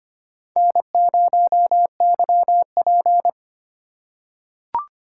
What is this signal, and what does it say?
Morse code, heard as a keyed tone, about 25 wpm, 700 Hz. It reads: N0YP
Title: USA Call Signs - 2x Character Spacing - 25wpm